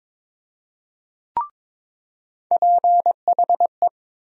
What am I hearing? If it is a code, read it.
Morse code, heard as a keyed tone, about 22 wpm, 700 Hz. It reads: PHE
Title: USA Call Sign Postfixes - 3 Letters 22wpm